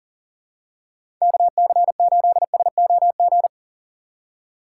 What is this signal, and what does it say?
Morse code, heard as a keyed tone, about 40 wpm, 700 Hz. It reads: KC8SOG
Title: USA Call Signs - 6 Characters - 3x Character Spacing - 40wpm